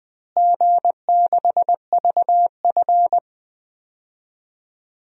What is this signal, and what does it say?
Morse code, heard as a keyed tone, about 21 wpm, 700 Hz. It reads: G6VF
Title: DXCC-100 Call Signs - 4 Characters - Repeated 3x - 20wpm